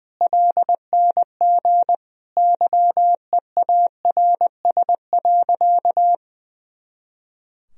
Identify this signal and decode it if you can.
Morse code, heard as a keyed tone, about 20 wpm, 700 Hz. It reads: LNG YEARS.
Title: Sentences from Top 200 Words and Abbreviations 20wpm